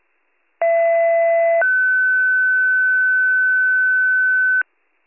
Two-tone page, heard as a one-second tone, then a three-second tone, about 670 then 1500 Hz, with noise. It is heard from 0.6 to 4.6 s.